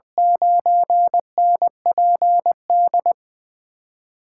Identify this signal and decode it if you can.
Morse code, heard as a keyed tone, about 20 wpm, 700 Hz. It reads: 9NPD
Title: USA Call Signs - 8x Character Spacing - 20wpm